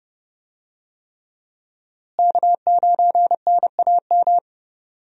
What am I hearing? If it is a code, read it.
Morse code, heard as a keyed tone, about 30 wpm, 700 Hz. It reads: K9NAM
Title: USA Call Signs - 5 Characters - Repeated 3x - 30wpm